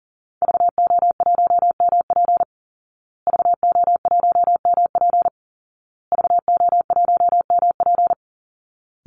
Morse code, heard as a keyed tone, about 40 wpm, 700 Hz. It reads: VO1MP VO1MP VO1MP
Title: Call Signs Repeated 2x - 40wpm